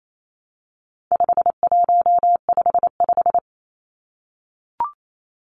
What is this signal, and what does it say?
Morse code, heard as a keyed tone, about 28 wpm, 700 Hz. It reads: 5155
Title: Numbers - 4-Digit 28wpm